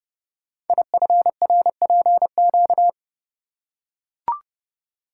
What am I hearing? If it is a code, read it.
Morse code, heard as a keyed tone, about 28 wpm, 700 Hz. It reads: IFRPQ
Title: Code Groups - 5 Letter:Numbers - 30wpm